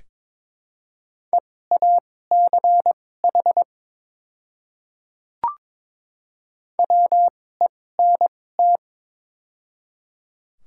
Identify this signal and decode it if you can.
Morse code, heard as a keyed tone, about 22 wpm, spaced as for 16 wpm, 700 Hz. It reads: EACH WENT
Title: 2x Character Spacing - Top 200 Words - 22wpm